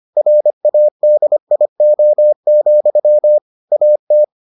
Morse code, heard as a keyed tone, about 25 wpm, 600 Hz. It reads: RADIO, AT